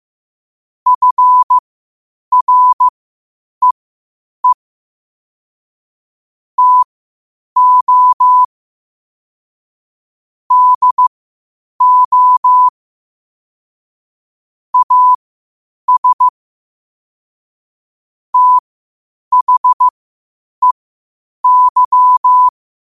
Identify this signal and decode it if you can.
Morse code, heard as a keyed tone, about 15 wpm, spaced as for 8 wpm, 1000 Hz. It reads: FREE TO DO AS THEY
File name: qotdfiles / 8wpm